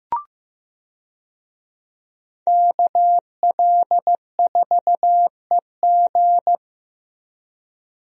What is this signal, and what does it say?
Morse code, heard as a keyed tone, about 15 wpm, 700 Hz. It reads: KL4EG
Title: USA Call Signs - 5 Characters - Repeated 2x - 15wpm